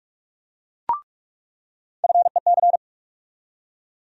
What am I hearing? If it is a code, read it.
Morse code, heard as a keyed tone, about 41 wpm, 700 Hz. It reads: FEC
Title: USA Call Sign Postfixes - 3 Letters 45wpm